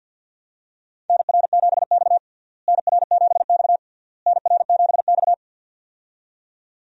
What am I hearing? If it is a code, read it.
Morse code, heard as a keyed tone, about 50 wpm, 700 Hz. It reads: NR7X NR7X NR7X
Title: USA Call Signs - 4 Characters - Repeated 3x - 50wpm